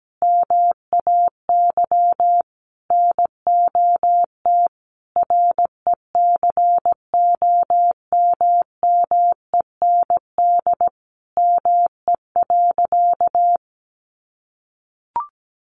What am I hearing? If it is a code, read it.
Morse code, heard as a keyed tone, about 17 wpm, 700 Hz. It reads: MAY NOT RECOMMEND ME.
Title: Sentences from Top 1200 Words 17wpm